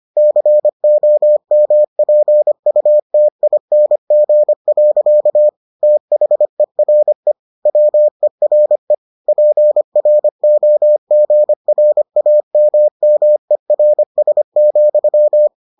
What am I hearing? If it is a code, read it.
Morse code, heard as a keyed tone, about 25 wpm, 600 Hz. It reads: COMPUTING. THERE WERE PROGRAMMERS,